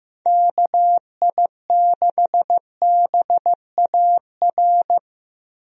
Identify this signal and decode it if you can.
Morse code, heard as a keyed tone, about 15 wpm, 700 Hz. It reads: KI6BAR
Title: USA Call Signs - 6 Characters - Repeated 2x - 15wpm